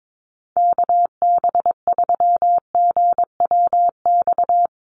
Morse code, heard as a keyed tone, about 22 wpm, 700 Hz. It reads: KB3GWX
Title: USA Call Signs 22wpm